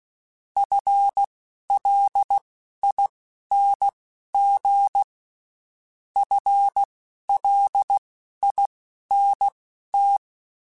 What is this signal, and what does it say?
Morse code, heard as a keyed tone, about 16 wpm, spaced as for 11 wpm, 800 Hz. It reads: FLING FLINT
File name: Spaced / FL Runs sp mp3s / FL Runs sp @15